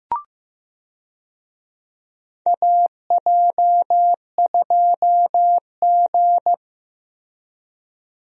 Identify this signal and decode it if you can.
Morse code, heard as a keyed tone, about 15 wpm, 700 Hz. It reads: AJ2G
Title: USA Call Signs - 4 Characters - Repeated 2x - 15wpm